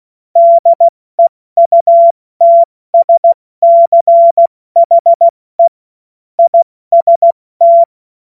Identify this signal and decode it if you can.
Morse code, heard as a keyed tone, about 16 wpm, 680 Hz. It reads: DEUTSCHE IST
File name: Morsekurs / Witze / Witz 30. Dezember 2025 70 F80